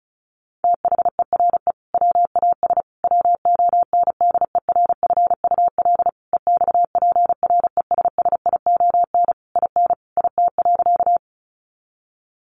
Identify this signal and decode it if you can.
Morse code, heard as a keyed tone, about 34 wpm, 700 Hz. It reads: THERE WAS WONDERFUL EXPRESSION IN IT.
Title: Sentences from Top 1200 Words 35wpm